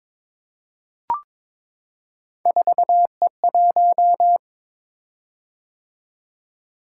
Morse code, heard as a keyed tone, about 22 wpm, 700 Hz. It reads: 4E1
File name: practice / 22wpm / Three Letter-Number Combinations - 22wpm